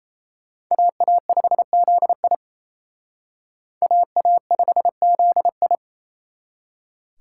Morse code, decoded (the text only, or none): AA5ZI AA5ZI